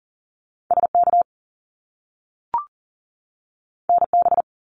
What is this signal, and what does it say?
Morse code, heard as a keyed tone, about 40 wpm, 700 Hz. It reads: SK NB
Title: Canadian Provinces and Territories 40wpm